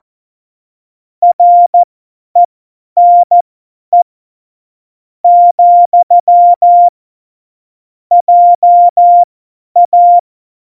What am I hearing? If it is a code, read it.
Morse code, heard as a keyed tone, about 14 wpm, spaced as for 10 wpm, 700 Hz. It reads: RENE , JA
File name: Morsekurs / Witze / Witz 20. Februar 2018, 50 F70